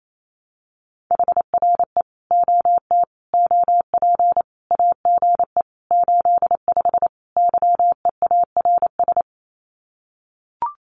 Morse code, heard as a keyed tone, about 28 wpm, 700 Hz. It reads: HRE OT OP AGE 85 YEARS
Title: Common QSO Phrases 28wpm